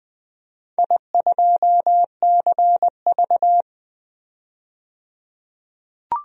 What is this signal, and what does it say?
Morse code, heard as a keyed tone, about 20 wpm, 700 Hz. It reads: I2CV
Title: DXCC-100 Call Signs - 4 Characters - 20wpm